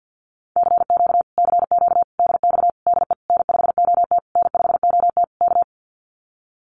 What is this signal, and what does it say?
Morse code, heard as a keyed tone, about 50 wpm, 700 Hz. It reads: CQ CQ DX DE N5OT N5OT K
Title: Common QSO Phrases Encore 50wpm